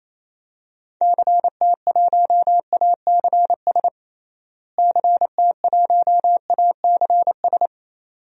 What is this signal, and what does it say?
Morse code, heard as a keyed tone, about 28 wpm, 700 Hz. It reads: CT1ACS CT1ACS
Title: DXCC-100 Call Signs - 6 Characters - Repeated 2x - 28wpm